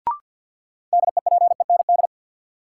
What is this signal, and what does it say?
Morse code, heard as a keyed tone, about 53 wpm, 700 Hz. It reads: DEPEND
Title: Sentences from Top 1300 Words - New Word Drill Alt - 50wpm